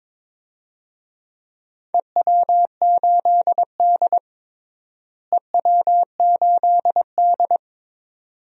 Morse code, heard as a keyed tone, about 22 wpm, 700 Hz. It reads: EW8D EW8D
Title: DXCC-100 Call Signs - 4 Characters - Repeated 2x - 22wpm